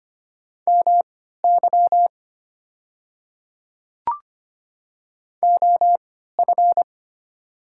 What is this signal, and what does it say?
Morse code, heard as a keyed tone, about 25 wpm, spaced as for 14 wpm, 700 Hz. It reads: MY OF
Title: 3x Character Spacing - Top 100 Words - 25wpm